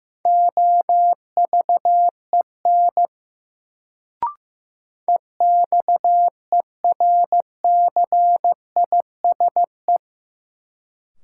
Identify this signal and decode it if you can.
Morse code, heard as a keyed tone, about 15 wpm, 700 Hz. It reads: OVEN EXERCISE
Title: Sentences from Top 1000 Words - Review Part 4 - New Word Drill - 15wpm